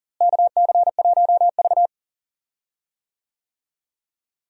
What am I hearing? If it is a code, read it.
Morse code, heard as a keyed tone, about 40 wpm, 700 Hz. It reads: KC1V
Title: USA Call Signs - 4 Characters - Repeated 2x - 40wpm